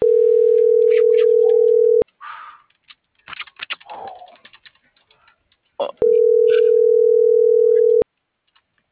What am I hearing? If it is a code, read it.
Ringback tone (North America), 440 + 480 Hz.